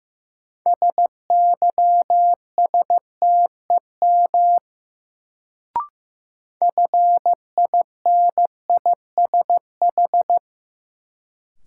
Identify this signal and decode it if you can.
Morse code, heard as a keyed tone, about 15 wpm, 700 Hz. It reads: SYSTEM FINISH
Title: Sentences from Top 1000 Words - Review Part 3 - New Word Drill - 15wpm